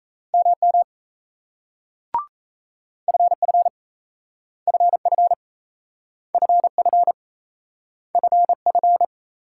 Morse code, heard as a keyed tone, about 38 wpm, 700 Hz. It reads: MM FF FF FF FF